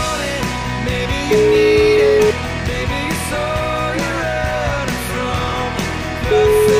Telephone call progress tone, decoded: ringback tone

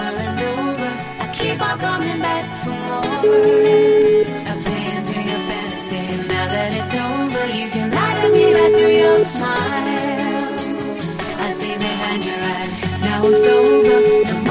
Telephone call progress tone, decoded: ringback tone

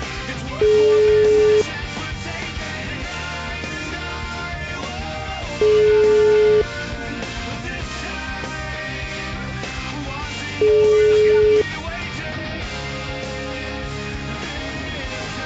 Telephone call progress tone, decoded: ringback tone

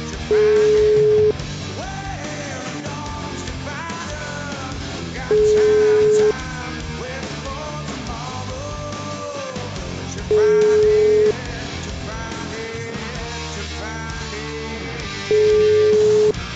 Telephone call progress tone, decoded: ringback tone

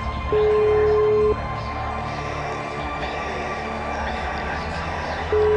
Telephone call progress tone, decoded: ringback tone

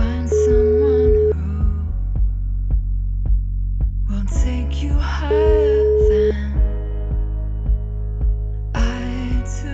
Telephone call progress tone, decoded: ringback tone

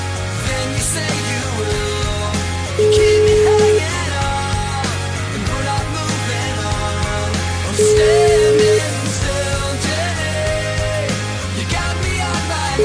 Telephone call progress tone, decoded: ringback tone

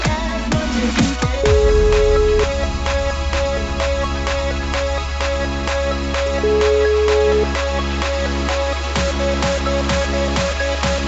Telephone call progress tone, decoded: ringback tone

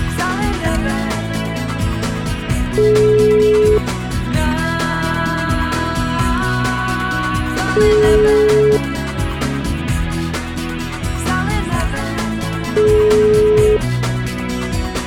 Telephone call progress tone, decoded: ringback tone

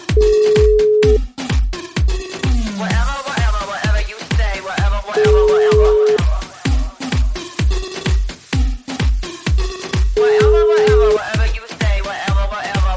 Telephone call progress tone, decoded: ringback tone

